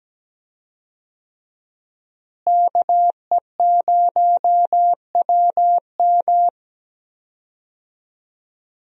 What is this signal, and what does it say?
Morse code, heard as a keyed tone, about 17 wpm, 700 Hz. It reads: KE0WM
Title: USA Call Signs - 5 Characters - Repeated 3x - 17wpm